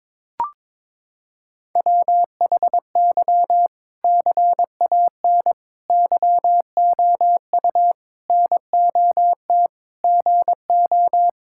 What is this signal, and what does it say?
Morse code, heard as a keyed tone, about 22 wpm, 700 Hz. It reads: WHY CAN YOU NOT GO
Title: Sentences from Top 1600 Words 22wpm